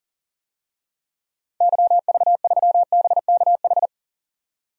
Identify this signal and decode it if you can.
Morse code, heard as a keyed tone, about 38 wpm, 700 Hz. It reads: YV3BKH
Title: DXCC-100 Call Signs - 6 Characters - 40wpm